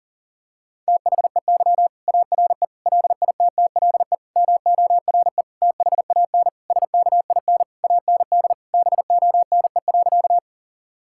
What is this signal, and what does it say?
Morse code, heard as a keyed tone, about 40 wpm, 700 Hz. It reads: THEY ARE LITTLE MORE THAN SKIN AND BONE.
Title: Sentences from Top 1600 Words 40wpm